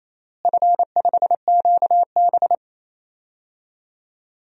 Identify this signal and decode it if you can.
Morse code, heard as a keyed tone, about 28 wpm, 700 Hz. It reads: F5QB
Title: DXCC-100 Call Signs - 4 Characters - Repeated 3x - 28wpm